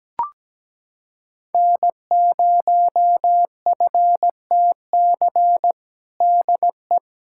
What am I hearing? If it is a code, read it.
Morse code, heard as a keyed tone, about 17 wpm, 700 Hz. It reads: N0FTC DE
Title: USA Callsign DE Callsign - 5 Characters - 17wpm